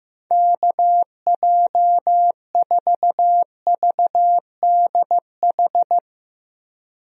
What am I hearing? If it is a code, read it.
Morse code, heard as a keyed tone, about 15 wpm, 700 Hz. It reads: KJ4VDH
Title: USA Call Signs - 6 Characters - Repeated 2x - 15wpm